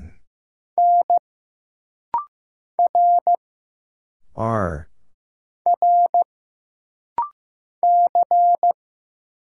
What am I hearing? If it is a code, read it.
Morse code, heard as a keyed tone, about 15 wpm, 700 Hz. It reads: N R R C
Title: Single Letters - Rapid-Fire 15wpm